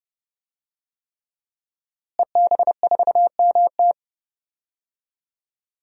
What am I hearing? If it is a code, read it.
Morse code, heard as a keyed tone, about 30 wpm, 700 Hz. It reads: EB4MT
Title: DXCC-100 Call Signs - 5 Characters - Repeated 3x - 30wpm